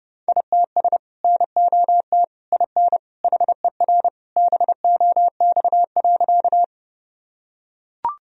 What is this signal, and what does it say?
Morse code, heard as a keyed tone, about 30 wpm, 700 Hz. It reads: ITS NOT IN HER BOX.
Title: Sentences from Top 1000 Words - Review Part 2 - 30wpm